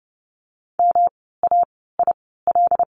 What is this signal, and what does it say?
Morse code, heard as a keyed tone, about 30 wpm, spaced as for 17 wpm, 700 Hz. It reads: MAIL
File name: course / 30:10wpm / 108 Lesson 18f - Letter m - Words 30wpm